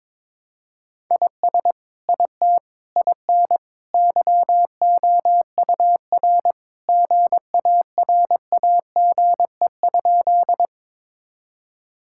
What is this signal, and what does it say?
Morse code, heard as a keyed tone, about 22 wpm, 700 Hz. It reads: IS IT IN YOUR GARAGE?
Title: Sentences from Top 1600 Words 22wpm